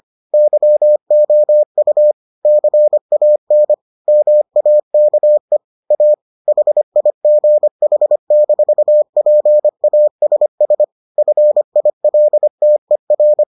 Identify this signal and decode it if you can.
Morse code, heard as a keyed tone, about 25 wpm, 600 Hz. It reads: YOU CAN MAKE A HIGH-PASS FILTER